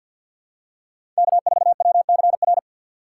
Morse code, heard as a keyed tone, about 52 wpm, 700 Hz. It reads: K4WCR
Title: USA Call Signs - 5 Characters - 4x Character Spacing - 50wpm